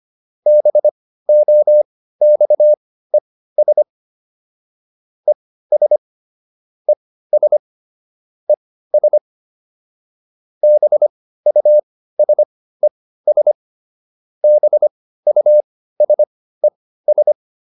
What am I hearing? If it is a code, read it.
Morse code, heard as a keyed tone, about 25 wpm, spaced as for 15 wpm, 600 Hz. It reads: BOXES ES ES ES BUSES BUSES